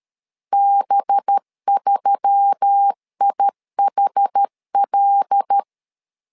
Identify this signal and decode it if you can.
Morse code, heard as a keyed tone, about 13 wpm, 800 Hz. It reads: B3IHL
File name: LEZIONI / 5 LBVC / 5 LB 12wpm 1906